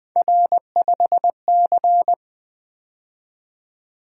Morse code, heard as a keyed tone, about 25 wpm, 700 Hz. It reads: R5C
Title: Speed-Racing - Three Letter-Number Combinations - 20wpm